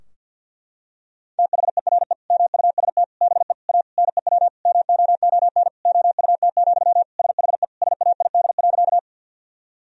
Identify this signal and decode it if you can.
Morse code, heard as a keyed tone, about 50 wpm, 700 Hz. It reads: THERE MUST BE A NEW MOON OUT, SHE SAID.
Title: Sentences from Top 1000 Words - Review Part 3 - 50wpm